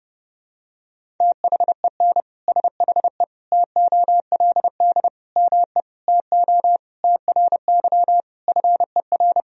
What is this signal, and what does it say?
Morse code, heard as a keyed tone, about 28 wpm, 700 Hz. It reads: THEN SHE TOLD ME TO TRY FER
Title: Sentences from Top 400 Words and Abbreviations 30wpm